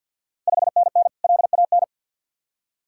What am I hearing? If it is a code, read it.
Morse code, heard as a keyed tone, about 51 wpm, 700 Hz. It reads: 5NN LAN
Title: Ontario QSO Party Exchange 50wpm